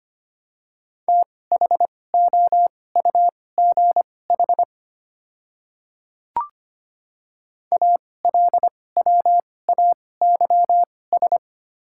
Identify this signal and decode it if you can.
Morse code, heard as a keyed tone, about 25 wpm, spaced as for 18 wpm, 700 Hz. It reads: THOUGH ALWAYS
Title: 2x Character Spacing - Top 500 Words - 25wpm